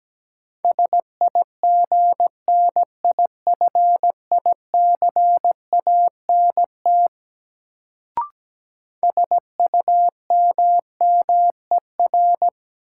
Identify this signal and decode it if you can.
Morse code, heard as a keyed tone, about 17 wpm, 700 Hz. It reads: SIGNIFICANT SUMMER
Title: Sentences from Top 1000 Words - Review Part 6 - New Word Drill - 17wpm